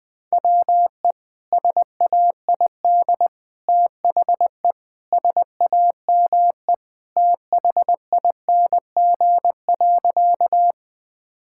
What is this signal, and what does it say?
Morse code, heard as a keyed tone, about 20 wpm, 700 Hz. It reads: WE SAID THE SAME THING.
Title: Sentences with 3-Word Phrases - 20wpm